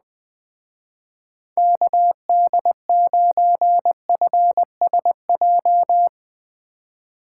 Morse code, heard as a keyed tone, about 20 wpm, 700 Hz. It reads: KD9FSJ
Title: Speed Racing - USA 2x3 Call Signs 17wpm